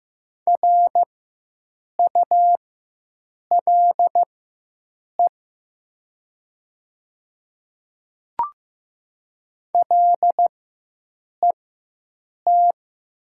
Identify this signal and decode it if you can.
Morse code, heard as a keyed tone, about 15 wpm, spaced as for 7 wpm, 690 Hz. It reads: RULE LET
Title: 4x Character Spacing - Top 500 Words - 15wpm